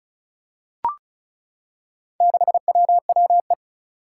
Morse code, heard as a keyed tone, about 36 wpm, 700 Hz. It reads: 6WWE
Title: Four Challenging Letter-Number Combinations 35wpm